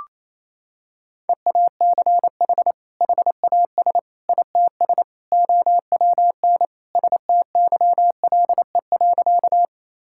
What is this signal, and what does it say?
Morse code, heard as a keyed tone, about 28 wpm, 700 Hz. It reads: EACH HAS ITS OWN STYLE.
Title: Sentences from Top 1000 Words - Review Part 5 - 28wpm